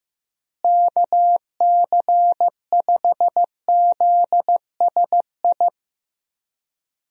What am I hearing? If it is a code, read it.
Morse code, heard as a keyed tone, about 15 wpm, 700 Hz. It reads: KC5ZSI
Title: USA Call Signs - 6 Characters - Repeated 2x - 15wpm